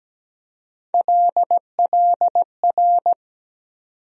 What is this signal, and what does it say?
Morse code, heard as a keyed tone, about 17 wpm, 700 Hz. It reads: LLR
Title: USA Call Sign Postfixes - 3 Letters 17wpm